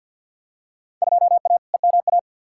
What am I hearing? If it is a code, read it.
Morse code, heard as a keyed tone, about 53 wpm, 700 Hz. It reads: 2A EMA
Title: Field Day Exchanges Encore - 50wpm